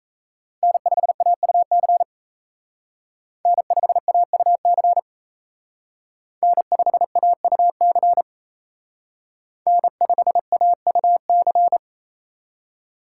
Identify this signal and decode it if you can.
Morse code, heard as a keyed tone, about 35 wpm, 700 Hz. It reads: N5AUC N5AUC N5AUC N5AUC
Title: Speed Racing - USA Call Signs 28wpm